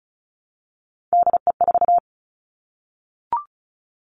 Morse code, heard as a keyed tone, about 36 wpm, 700 Hz. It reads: DE4
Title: Three Challenging Letter-Number Combinations 35wpm